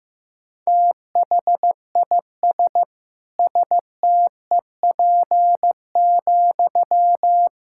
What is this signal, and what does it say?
Morse code, heard as a keyed tone, about 15 wpm, 700 Hz. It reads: THIS STEP,
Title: Sentences from Top 1000 Words - Review Part 6 - 15wpm